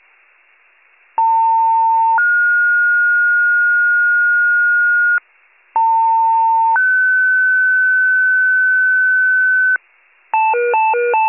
Two-tone page, heard as a one-second tone, then a three-second tone, about 900 then 1430 Hz, about 900 then 1500 Hz. It is heard from 1.2 to 9.8 s.